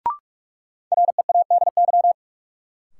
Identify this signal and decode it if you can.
Morse code, heard as a keyed tone, about 43 wpm, 700 Hz. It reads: READY
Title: Sentences from Top 400 Words - New Word Drill - 45wpm